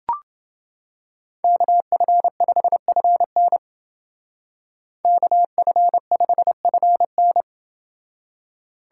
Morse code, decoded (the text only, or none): KF5FN KF5FN